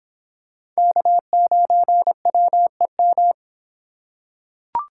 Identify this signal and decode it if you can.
Morse code, heard as a keyed tone, about 26 wpm, 700 Hz. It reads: K9WEM